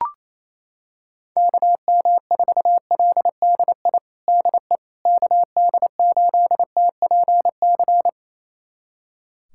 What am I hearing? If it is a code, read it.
Morse code, heard as a keyed tone, about 28 wpm, 700 Hz. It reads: KM4LDI DE KD8TPC
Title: USA Callsign DE Callsign - 6 Characters - 28wpm